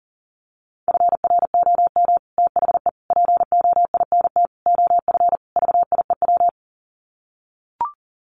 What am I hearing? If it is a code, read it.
Morse code, heard as a keyed tone, about 40 wpm, 700 Hz. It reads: FROM THE POINT OF VIEW
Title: Sets of 5 Words 40wpm